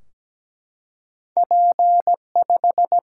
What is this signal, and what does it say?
Morse code, heard as a keyed tone, about 17 wpm, 700 Hz. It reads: P5
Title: Single-Letter Single-Number 17wpm